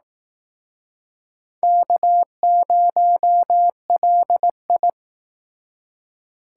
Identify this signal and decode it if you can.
Morse code, heard as a keyed tone, about 19 wpm, 700 Hz. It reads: K0LI